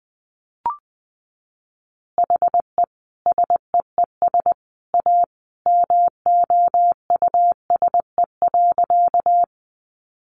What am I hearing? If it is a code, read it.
Morse code, heard as a keyed tone, about 20 wpm, 700 Hz. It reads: HE SEES A MOUSE.
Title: Easy Phrases 20wpm